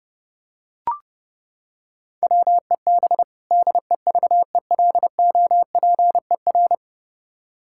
Morse code, heard as a keyed tone, about 32 wpm, 700 Hz. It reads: WEB DEVELOPER
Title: K1USN SST QSO Practice 30wpm